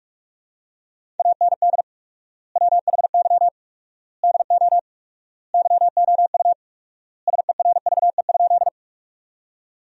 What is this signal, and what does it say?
Morse code, heard as a keyed tone, about 43 wpm, 700 Hz. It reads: AND WHY DO YOU SERVE?
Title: 4x Word-Spacing - Sentences from Top 400 Words - 45wpm